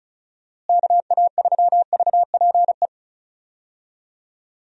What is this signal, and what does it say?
Morse code, heard as a keyed tone, about 36 wpm, 700 Hz. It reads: KA3VPE
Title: USA Call Signs - 6 Characters - Repeated 3x - 35wpm